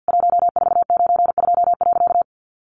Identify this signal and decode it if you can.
Morse code, heard as a keyed tone, about 50 wpm, 700 Hz. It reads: 1492.
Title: A Sound Of Thunder 50wpm